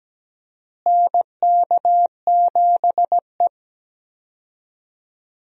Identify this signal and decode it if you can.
Morse code, heard as a keyed tone, about 17 wpm, 700 Hz. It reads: NK7E
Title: USA Call Signs - 4 Characters - Repeated 2x - 17wpm